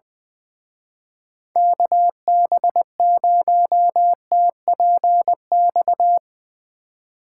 Morse code, heard as a keyed tone, about 21 wpm, 700 Hz. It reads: KB0TPX
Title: Speed Racing - USA 2x3 Call Signs 20wpm